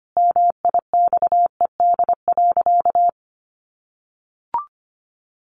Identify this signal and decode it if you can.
Morse code, heard as a keyed tone, about 25 wpm, 700 Hz. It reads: MIXED.
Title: Sentences from Top 1500 Words 25wpm